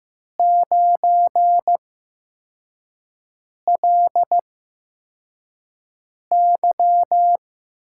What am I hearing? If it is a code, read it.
Morse code, heard as a keyed tone, about 15 wpm, spaced as for 4 wpm, 700 Hz. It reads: 9LY
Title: USA Call Signs - 6 Characters - 8x Character Spacing - 15wpm